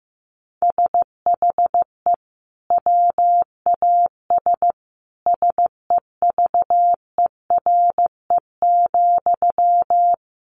Morse code, heard as a keyed tone, about 15 wpm, 700 Hz. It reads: SHE WAS SEVERE,
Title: Sentences from Top 1400 Words 15wpm